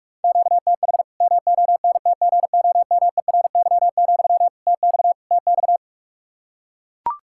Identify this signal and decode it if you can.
Morse code, heard as a keyed tone, about 41 wpm, 700 Hz. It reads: QTH MONTGOMERY, TX TX
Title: Practice QSO - Exchange 1 - Beginning - 45wpm